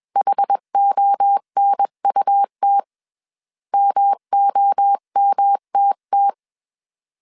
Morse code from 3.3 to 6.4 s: MOMTT